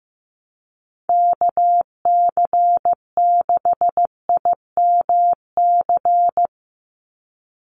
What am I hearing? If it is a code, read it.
Morse code, heard as a keyed tone, about 15 wpm, 700 Hz. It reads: KC6IMC
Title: USA Call Signs 15wpm